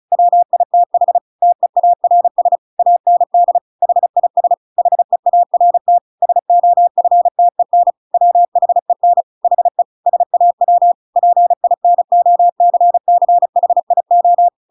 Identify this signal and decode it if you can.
Morse code, heard as a keyed tone, about 34 wpm, 700 Hz. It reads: WITH TEARS AND HIS HEART SOFTEN WHEN HE SAW PINOCCHIO
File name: Pinocchio / Pinocchio ch8 35 35 ik1hge